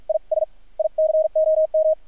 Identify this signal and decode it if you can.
Morse code, heard as a keyed tone, about 53 wpm, spaced as for 43 wpm, 630 Hz, with radio noise. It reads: IS ICOM
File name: archived / CRSnet